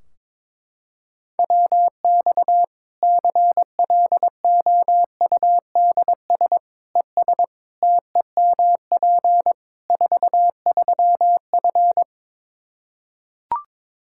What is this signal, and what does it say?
Morse code, heard as a keyed tone, about 22 wpm, 700 Hz. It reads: WX CLOUDS ES TEMP 43F
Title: Practice QSO - Exchange 2 - Info - 22wpm